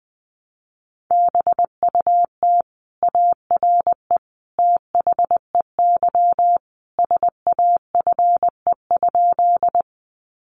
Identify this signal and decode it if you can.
Morse code, heard as a keyed tone, about 20 wpm, 700 Hz. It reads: BUT ARE THEY SAFE?
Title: Sentences from Top 900 Words 20wpm